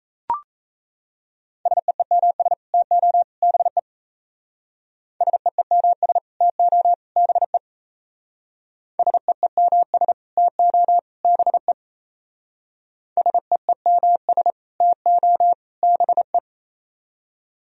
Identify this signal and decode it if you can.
Morse code, heard as a keyed tone, about 35 wpm, 700 Hz. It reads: SEEMS TO BE SEEMS TO BE SEEMS TO BE SEEMS TO BE
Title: Speed-Racing - Sets of 3 Words 28wpm